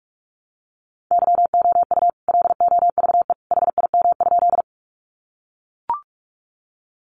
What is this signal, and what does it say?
Morse code, heard as a keyed tone, about 44 wpm, 700 Hz. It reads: YOU LOVE HIM?
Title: Sentences from Top 400 Words 45wpm